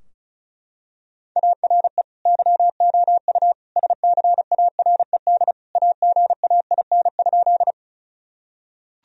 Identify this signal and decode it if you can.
Morse code, heard as a keyed tone, about 36 wpm, 700 Hz. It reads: ARE YOU SCARED AGAIN?
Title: Sentences from Top 1000 Words - Review Part 4 - 35wpm